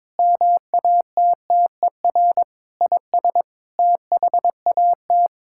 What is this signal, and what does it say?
Morse code, heard as a keyed tone, about 22 wpm, 700 Hz. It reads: MATTER IS THAT
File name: Sets of 9 Words - 22wpm